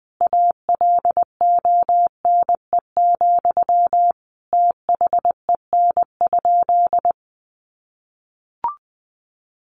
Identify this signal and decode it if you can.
Morse code, heard as a keyed tone, about 20 wpm, 700 Hz. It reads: ALONE, THEN?
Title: Sentences from Top 1000 Words 20wpm